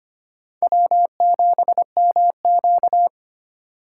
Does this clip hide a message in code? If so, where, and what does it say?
Yes: Morse code from 0.6 to 3.1 s, heard as a keyed tone, about 33 wpm, 700 Hz. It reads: W7MQ